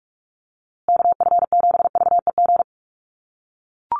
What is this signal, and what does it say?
Morse code, heard as a keyed tone, about 44 wpm, 700 Hz. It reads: KF7VEG
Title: USA Call Signs Repeated 2x 45wpm